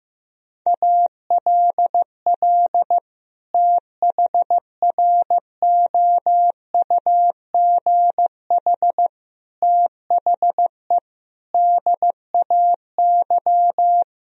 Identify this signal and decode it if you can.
Morse code, heard as a keyed tone, about 15 wpm, 700 Hz. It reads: ALL THROUGH THE DAY
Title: Sentences from Top 1000 Words - Review Part 1 - 15wpm